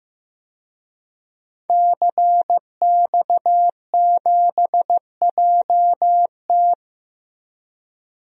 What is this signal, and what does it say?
Morse code, heard as a keyed tone, about 15 wpm, 700 Hz. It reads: CX7JT
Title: DXCC-100 Call Signs - 5 Characters - 15wpm